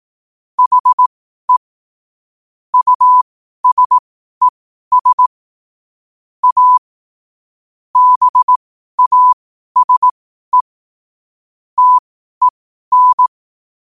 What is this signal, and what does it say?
Morse code, heard as a keyed tone, about 18 wpm, spaced as for 12 wpm, 1000 Hz. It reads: HE USES A BASE TEN
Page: 12 wpm morse code quotes